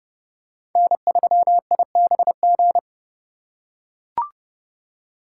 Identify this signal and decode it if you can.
Morse code, heard as a keyed tone, about 30 wpm, 700 Hz. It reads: N3IBG
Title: USA Call Signs - 5 Characters - 3x Character Spacing - 30wpm